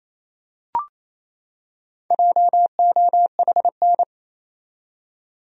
Morse code, heard as a keyed tone, about 28 wpm, 700 Hz. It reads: JOHN
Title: USA Common Names - 28wpm